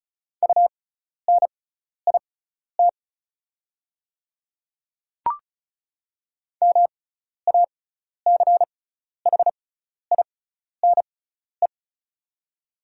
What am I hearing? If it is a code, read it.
Morse code, heard as a keyed tone, about 36 wpm, spaced as for 12 wpm, 690 Hz. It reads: UNIT MACHINE
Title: 6x Character Spacing - Top 500 Words - 35wpm